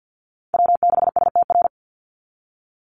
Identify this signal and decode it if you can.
Morse code, heard as a keyed tone, about 48 wpm, 700 Hz. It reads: W6STR
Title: NA SOTA Chaser Callsigns 50wpm